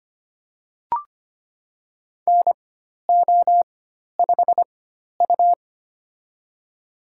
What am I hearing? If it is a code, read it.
Morse code, heard as a keyed tone, about 25 wpm, spaced as for 12 wpm, 700 Hz. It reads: NO5U
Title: USA Call Signs - 4 Characters - 4x Character Spacing - 25wpm